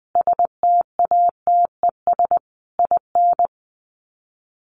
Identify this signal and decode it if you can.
Morse code, heard as a keyed tone, about 20 wpm, 700 Hz. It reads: STATES IN
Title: Sets of 4 Words 20wpm